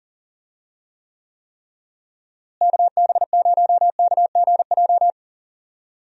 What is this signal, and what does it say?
Morse code, heard as a keyed tone, about 39 wpm, 700 Hz. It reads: KB0KGJ
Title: USA Call Signs - 6 Characters - Repeated 3x - 40wpm